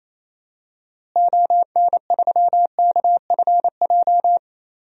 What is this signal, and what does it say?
Morse code, heard as a keyed tone, about 28 wpm, 700 Hz. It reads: ON3KFJ
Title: DXCC-100 Call Signs - 6 Characters - Repeated 3x - 28wpm